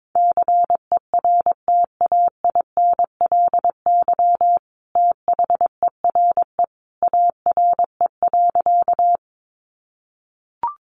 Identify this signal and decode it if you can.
Morse code, heard as a keyed tone, about 22 wpm, 700 Hz. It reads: CERTAINLY THERE ARE.
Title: Sentences from Top 900 Words 22wpm